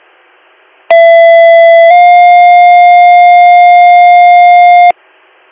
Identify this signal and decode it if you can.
Two-tone page, heard as a one-second tone, then a three-second tone, about 670 then 710 Hz. It is heard from 0.9 to 4.9 s.